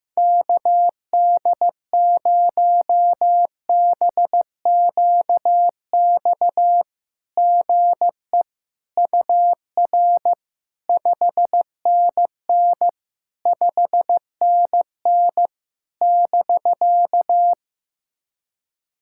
Morse code, heard as a keyed tone, about 15 wpm, 700 Hz. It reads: KD0BQX GE UR 5NN 5NN <BK>
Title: SOTA QSO Practice 15wpm